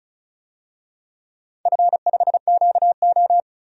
Morse code, heard as a keyed tone, about 36 wpm, 690 Hz. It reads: F5QO
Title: DXCC-100 Call Signs - 4 Characters - 35wpm